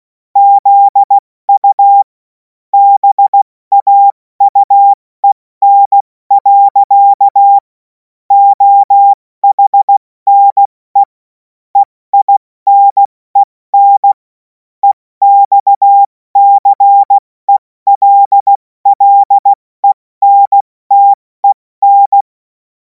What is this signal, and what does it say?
Morse code, heard as a keyed tone, about 16 wpm, 800 Hz. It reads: ZU BAUEN. OHNE EINEN EXCELLENTEN
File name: Morsekurs / Witze / Witz 24. Februar 2026 70 F80